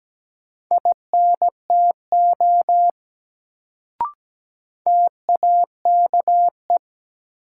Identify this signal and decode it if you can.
Morse code, heard as a keyed tone, about 17 wpm, 700 Hz. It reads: INTO TAKE
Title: Sets of 2 Words - New Word Drill - 17wpm